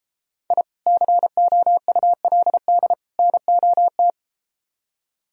Morse code, heard as a keyed tone, about 33 wpm, 700 Hz. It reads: I COULD NOT